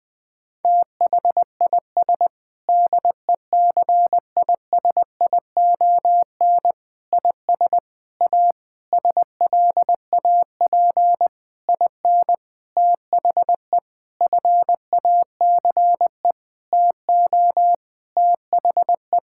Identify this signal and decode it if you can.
Morse code, heard as a keyed tone, about 20 wpm, 700 Hz. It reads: THIS DECISION IS A SLAP IN THE FACE TO THE
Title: Sentences with 8-Word Phrases - Spoken Beginning - 20wpm